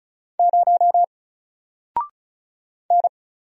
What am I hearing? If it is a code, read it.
Morse code, heard as a keyed tone, about 36 wpm, 700 Hz. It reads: 0 N
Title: Single Letter-Number Alt 35wpm